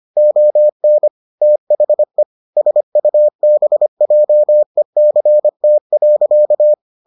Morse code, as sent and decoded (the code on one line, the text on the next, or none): --- -. / - .... / ... ..- -... .--- . -.-. - .-.-.-
ON THE SUBJECT.